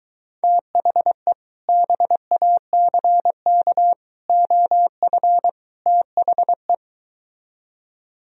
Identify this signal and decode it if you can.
Morse code, heard as a keyed tone, about 23 wpm, 700 Hz. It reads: THE BACK OF THE